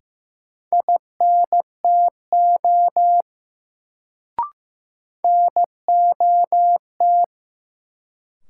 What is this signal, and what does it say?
Morse code, heard as a keyed tone, about 16 wpm, 700 Hz. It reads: INTO NOT
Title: Sentences from Top 600 Words - New Word Drill - 15wpm